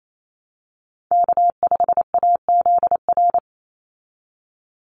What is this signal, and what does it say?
Morse code, heard as a keyed tone, about 28 wpm, 700 Hz. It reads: K5AZR
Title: Call Signs - 28wpm